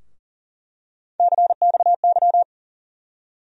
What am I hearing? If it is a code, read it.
Morse code, heard as a keyed tone, about 40 wpm, 700 Hz. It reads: CXY
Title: USA Call Sign Postfixes - 3 Letters 40wpm